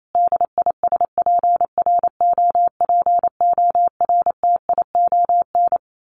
Morse code, heard as a keyed tone, about 28 wpm, 700 Hz. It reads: DISPROPORTION
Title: A Sound Of Thunder 28wpm